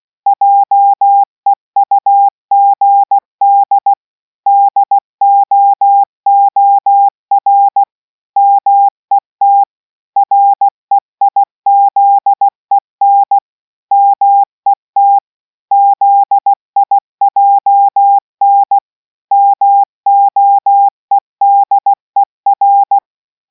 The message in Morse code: JEUGD DOOR MET REIZEN MET ZIJN MOEDER